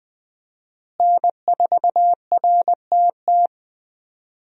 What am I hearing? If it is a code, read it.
Morse code, heard as a keyed tone, about 20 wpm, 700 Hz. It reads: N4RTT
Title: USA Call Signs - 5 Characters - Repeated 2x - 20wpm